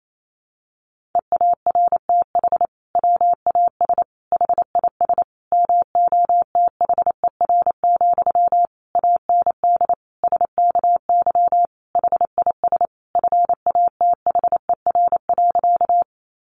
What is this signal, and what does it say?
Morse code, heard as a keyed tone, about 28 wpm, 700 Hz. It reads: EARTH WAS HIS MOTHER, AND SKY HIS FATHER.
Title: Sentences from Top 1200 Words 28wpm